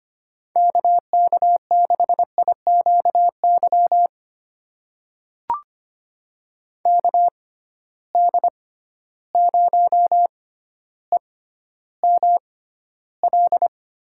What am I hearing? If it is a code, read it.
Morse code, heard as a keyed tone, about 25 wpm, spaced as for 13 wpm, 700 Hz. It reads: KK6IQY KD0EML
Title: USA Call Signs - 6 Characters - 6x Character Spacing - 25wpm